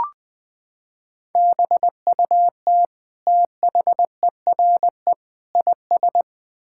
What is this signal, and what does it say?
Morse code, heard as a keyed tone, about 21 wpm, 700 Hz. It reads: BUT THERE IS
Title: Sentences with 9-Word Phrases - 20wpm